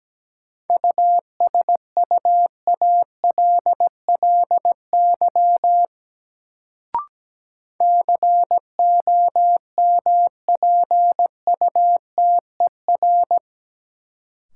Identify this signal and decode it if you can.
Morse code, heard as a keyed tone, about 17 wpm, 700 Hz. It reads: USUALLY COMPUTER
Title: Sentences from Top 1000 Words - Review Part 3 - New Word Drill - 17wpm